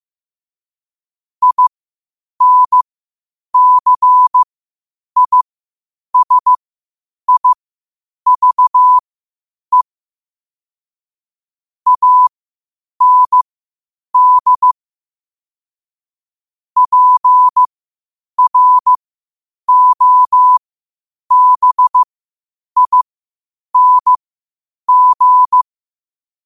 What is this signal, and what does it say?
Morse code, heard as a keyed tone, about 15 wpm, spaced as for 8 wpm, 1000 Hz. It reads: INCISIVE AND PROBING
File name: 8wpm